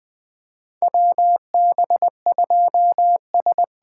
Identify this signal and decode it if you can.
Morse code, heard as a keyed tone, about 20 wpm, 700 Hz. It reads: WB2S
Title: USA Call Signs - 6 Characters - 8x Character Spacing - 20wpm